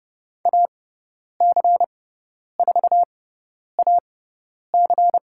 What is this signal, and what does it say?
Morse code, heard as a keyed tone, about 30 wpm, spaced as for 10 wpm, 700 Hz. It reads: AC4AC
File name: Lesson 18g - Letter m - Call Signs 30wpm